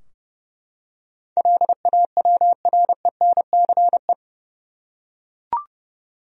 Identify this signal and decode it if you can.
Morse code, heard as a keyed tone, about 31 wpm, 700 Hz. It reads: LAWRENCE
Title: USA Common Names - 30wpm